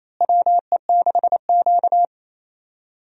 Morse code, heard as a keyed tone, about 28 wpm, 700 Hz. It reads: WE6Q
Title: USA Call Signs - 4 Characters - Repeated 2x - 28wpm